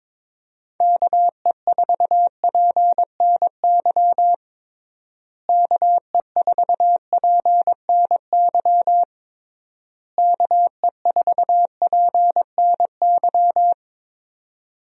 Morse code, heard as a keyed tone, about 22 wpm, 700 Hz. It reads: KE4PNY KE4PNY KE4PNY
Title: USA Call Signs - 6 Characters - Repeated 3x - 22wpm